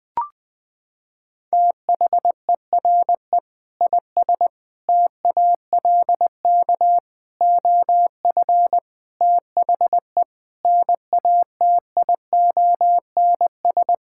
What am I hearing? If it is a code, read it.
Morse code, heard as a keyed tone, about 20 wpm, 700 Hz. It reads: THERE IS TALK OF THE NATIONS
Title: Sentences with 5-Word Phrases - 20wpm